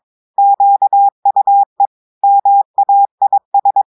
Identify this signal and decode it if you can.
Morse code, heard as a keyed tone, about 22 wpm, 800 Hz. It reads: QUE MAIS
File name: audios / portugais / portugais texte en clair 1 vitesse 22 mots minutes